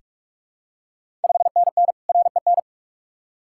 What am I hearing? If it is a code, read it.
Morse code, heard as a keyed tone, about 46 wpm, 690 Hz. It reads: 5NN REN
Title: Ontario QSO Party Exchange 45wpm